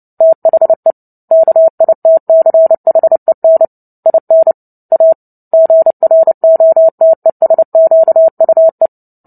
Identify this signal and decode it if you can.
Morse code, heard as a keyed tone, about 30 wpm, 650 Hz. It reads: THE KITCHEN IN A GROTESQUE
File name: Book Two / 03-Chapter 3 - 29 WPM